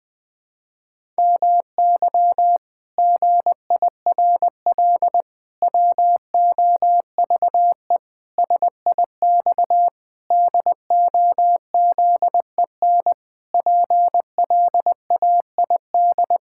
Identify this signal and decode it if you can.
Morse code, heard as a keyed tone, about 20 wpm, 700 Hz. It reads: MY GIRL WOVE SIX DOZEN PLAID
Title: Pangram Challenge v2 - 20wpm